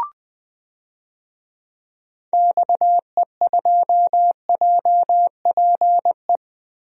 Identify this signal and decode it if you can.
Morse code, heard as a keyed tone, about 21 wpm, 700 Hz. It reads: XE2JPE
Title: DXCC-100 Call Signs - 6 Characters - Repeated 2x - 20wpm